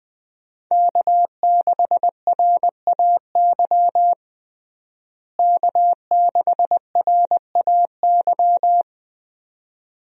Morse code, heard as a keyed tone, about 20 wpm, 700 Hz. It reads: K6RAY K6RAY
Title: USA Call Signs - 5 Characters - Repeated 2x - 20wpm